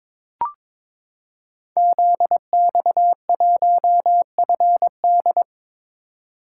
Morse code, heard as a keyed tone, about 22 wpm, 700 Hz. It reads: ZX1FD
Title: Code Groups - 5 Letter:Numbers - 22wpm